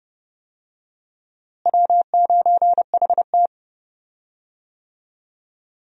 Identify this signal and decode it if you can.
Morse code, heard as a keyed tone, about 30 wpm, 700 Hz. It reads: W9HT
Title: USA Call Signs - 4 Characters - Repeated 3x - 30wpm